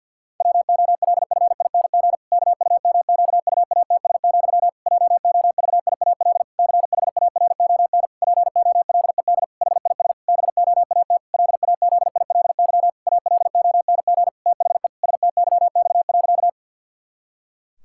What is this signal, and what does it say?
Morse code, heard as a keyed tone, about 50 wpm, 700 Hz. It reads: WOLFING KUMQUATS, JOVIAL CHARON POLED HIS BOAT LAZILY ALONG THE STYX.
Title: Pangram Challenge v2 - 50wpm